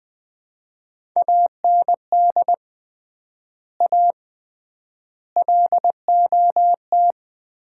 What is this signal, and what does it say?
Morse code, heard as a keyed tone, about 20 wpm, 700 Hz. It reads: AND A LOT
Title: 3x Word-Spacing - Sets of 5 Words - 20wpm